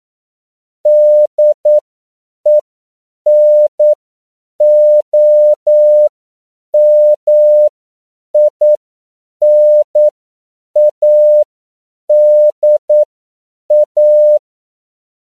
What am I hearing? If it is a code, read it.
Morse code, heard as a keyed tone, about 9 wpm, spaced as for 7 wpm, 600 Hz. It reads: DENOMINADA
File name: CW / 10PPM